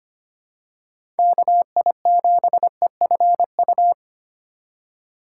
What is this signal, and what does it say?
Morse code, heard as a keyed tone, about 25 wpm, 700 Hz. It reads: KI7EFU
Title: USA Call Signs - 3x Character Spacing - 25wpm